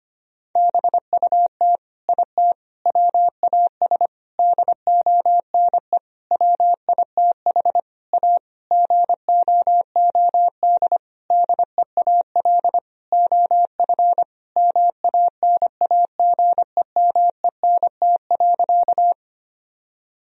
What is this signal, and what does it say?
Morse code, heard as a keyed tone, about 25 wpm, 700 Hz. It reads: BUT IT WAS DONE WITH A GOOD DEAL OF MANAGEMENT.
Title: Sentences from Top 1000 Words - Review Part 4 - 25wpm